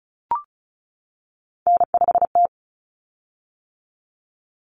Morse code, heard as a keyed tone, about 34 wpm, 700 Hz. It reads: N5T
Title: Common Cut Numbers - 3-Digit 35wpm